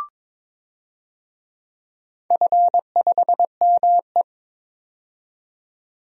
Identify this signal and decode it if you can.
Morse code, heard as a keyed tone, about 22 wpm, 700 Hz. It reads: F5ME
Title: DXCC-100 Call Signs - 4 Characters - 22wpm